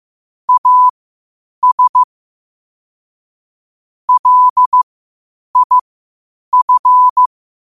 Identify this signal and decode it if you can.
Morse code, heard as a keyed tone, about 15 wpm, spaced as for 8 wpm, 1000 Hz. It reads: AS LIF
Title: 8 WPM morse code quotes for Fri, 01 Mar 2024